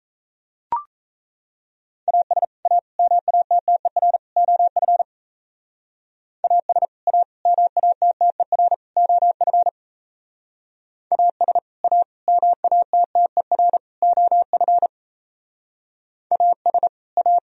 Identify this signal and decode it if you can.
Morse code, heard as a keyed tone, about 32 wpm, 700 Hz. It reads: AS A MATTER OF AS A MATTER OF AS A MATTER OF AS A
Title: Speed-Racing - Sets of 4 Words 28wpm